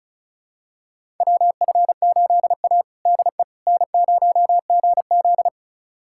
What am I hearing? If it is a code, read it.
Morse code, heard as a keyed tone, about 34 wpm, 700 Hz. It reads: WF8A DE N0GZ
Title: USA Callsign DE Callsign - 4 Characters - 35wpm